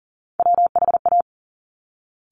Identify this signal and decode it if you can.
Morse code, heard as a keyed tone, about 43 wpm, 700 Hz. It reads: WHA
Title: Top Three Letter Combinations in Words 40wpm